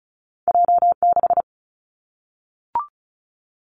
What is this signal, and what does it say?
Morse code, heard as a keyed tone, about 36 wpm, 700 Hz. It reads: J 6
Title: Two Challenging Letter-Number Combinations 35wpm